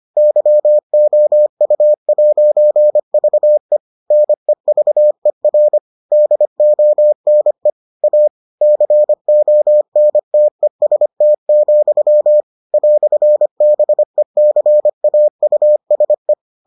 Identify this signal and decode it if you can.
Morse code, heard as a keyed tone, about 25 wpm, 600 Hz. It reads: YOU'VE NEVER DONE A CONTEST, "BECAUSE